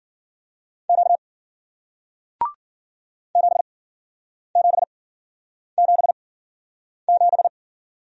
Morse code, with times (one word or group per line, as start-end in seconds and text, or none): 0.9-1.2 Q
3.4-7.5 7777